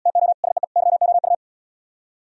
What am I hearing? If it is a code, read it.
Morse code, heard as a keyed tone, about 77 wpm, 700 Hz. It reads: TO SEE YOU